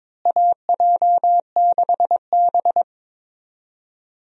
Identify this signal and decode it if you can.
Morse code, heard as a keyed tone, about 22 wpm, 700 Hz. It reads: AJ6B
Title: USA Call Signs - 4 Characters - Repeated 3x - 22wpm